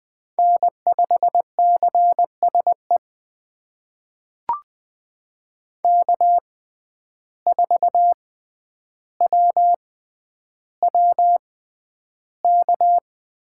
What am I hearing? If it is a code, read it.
Morse code, heard as a keyed tone, about 20 wpm, spaced as for 10 wpm, 700 Hz. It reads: N5CSE K4WWK
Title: USA Call Signs - 5 Characters - 6x Character Spacing - 20wpm